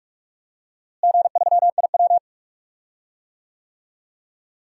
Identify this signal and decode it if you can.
Morse code, heard as a keyed tone, about 42 wpm, 700 Hz. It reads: G3IW
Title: DXCC-100 Call Signs - 4 Characters - Repeated 3x - 45wpm